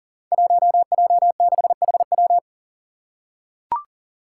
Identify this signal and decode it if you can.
Morse code, heard as a keyed tone, about 40 wpm, 700 Hz. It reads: 1J6HW
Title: Five Challenging Letter-Number Combinations 40wpm